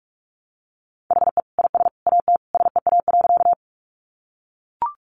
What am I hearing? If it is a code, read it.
Morse code, heard as a keyed tone, about 44 wpm, 700 Hz. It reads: HE IS AT SEA.
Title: Sentences from Top 300 Words 45wpm